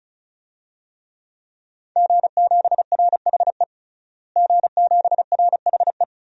Morse code, heard as a keyed tone, about 36 wpm, 700 Hz. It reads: G7RHE G7RHE
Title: DXCC-100 Call Signs - 5 Characters - Repeated 3x - 35wpm